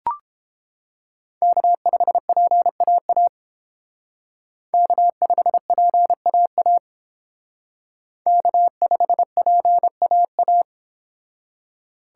Morse code, heard as a keyed tone, about 30 wpm, 700 Hz. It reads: K5PAA K5PAA K5PAA